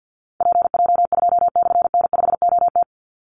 Morse code, heard as a keyed tone, about 50 wpm, 700 Hz. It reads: PJ2/N5OT
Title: Call Signs - 50wpm